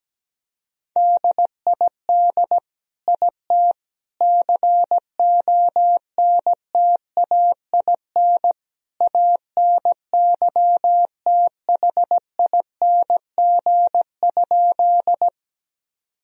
Morse code, heard as a keyed tone, about 17 wpm, 700 Hz. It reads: DID IT CONTAIN ANYTHING?
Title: Sentences from Top 1000 Words - Review Part 2 - 17wpm